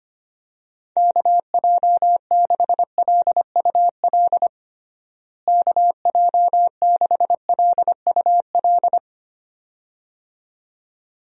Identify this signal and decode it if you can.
Morse code, heard as a keyed tone, about 25 wpm, 700 Hz. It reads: KJ6LUL KJ6LUL
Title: USA Call Signs - 6 Characters - Repeated 2x - 25wpm